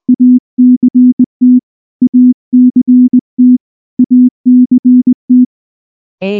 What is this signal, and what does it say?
Morse code, heard as a keyed tone, about 19 wpm, 250 Hz. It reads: ACT ACT ACT
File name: morse / morse-training-20WPM-8